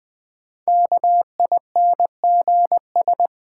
Morse code, heard as a keyed tone, about 20 wpm, 700 Hz. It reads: KINGS
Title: Sentences from Top 1000 Words - Review Part 6 - New Word Drill - 20wpm